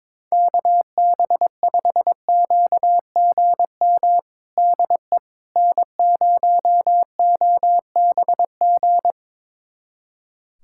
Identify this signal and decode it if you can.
Morse code, heard as a keyed tone, about 22 wpm, 700 Hz. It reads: KB5QGM DE N0OBG
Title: USA Callsign DE Callsign 22wpm